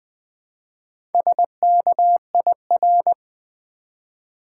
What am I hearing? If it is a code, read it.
Morse code, heard as a keyed tone, about 20 wpm, 700 Hz. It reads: SKIR
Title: Build A Word - 1 Letter At A Time - 3 to 5-Letter Words - 20wpm